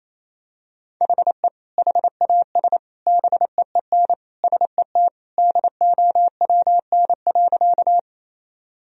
Morse code, heard as a keyed tone, about 28 wpm, 700 Hz. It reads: HE HAS BEEN SET DOWN.
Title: Sentences from Top 1000 Words - Review Part 1 - 28wpm